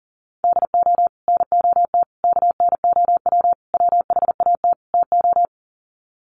Morse code, heard as a keyed tone, about 40 wpm, 700 Hz. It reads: DO NOT KNOW WHAT TO
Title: Sets of 5 Words 40wpm